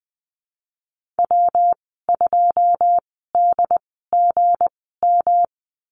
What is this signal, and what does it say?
Morse code, heard as a keyed tone, about 19 wpm, 700 Hz. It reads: W 2 D G M
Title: 231 Lesson 37g - Letter x - Call Signs 20wpm